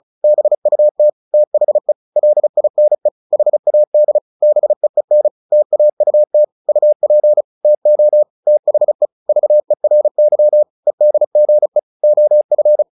Morse code, heard as a keyed tone, about 34 wpm, 600 Hz. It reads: BUT THE LINE HAD BEEN TAUT UP TO THE VERY EDGE OF
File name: the old man and the sea / the old man and the sea 1 35wpm